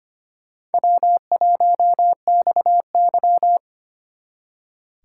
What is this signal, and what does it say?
Morse code, heard as a keyed tone, about 32 wpm, 700 Hz. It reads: W1XY